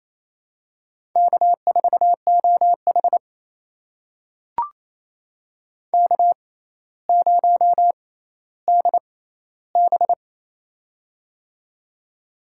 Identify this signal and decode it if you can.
Morse code, heard as a keyed tone, about 27 wpm, spaced as for 14 wpm, 700 Hz. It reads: K4OH K0DB
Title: USA Call Signs - 4 Characters - 6x Character Spacing - 28wpm